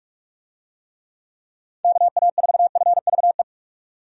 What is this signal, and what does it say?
Morse code, heard as a keyed tone, about 46 wpm, 700 Hz. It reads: KA4FVE
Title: USA Call Signs - 6 Characters - 45wpm